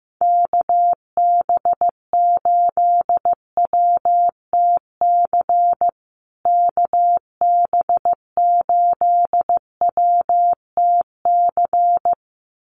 Morse code, heard as a keyed tone, about 15 wpm, 700 Hz. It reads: KB8WTC KB8WTC
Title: USA Call Signs Repeated 2x 15wpm